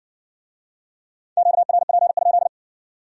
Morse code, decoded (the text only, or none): QRP?